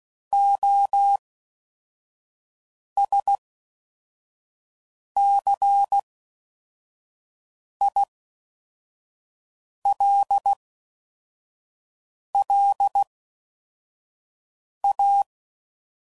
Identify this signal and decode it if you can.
Morse code, heard as a keyed tone, about 16 wpm, 800 Hz. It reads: O S C I L L A
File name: Suffixes / Special INGs / Fast INGs